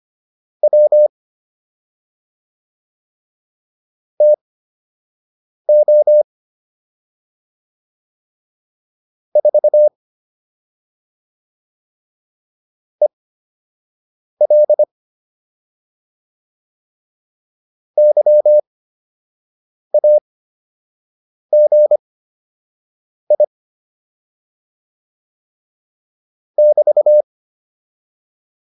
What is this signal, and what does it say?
Morse code, heard as a keyed tone, about 25 wpm, spaced as for 6 wpm, 600 Hz. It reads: W TO 4 EL YAGI <BT>